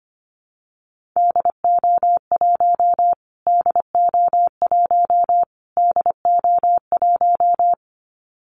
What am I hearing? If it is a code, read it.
Morse code, heard as a keyed tone, about 25 wpm, 700 Hz. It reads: DO1 DO1 DO1
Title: Callsign Prefixes for Top 25 DXCC - Repeated 3x - 25wpm